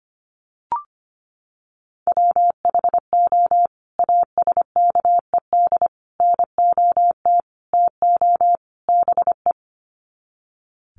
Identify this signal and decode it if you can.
Morse code, heard as a keyed tone, about 25 wpm, 700 Hz. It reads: WHO ASKED NOT TO BE
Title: Sets of 5 Words Encore2 25wpm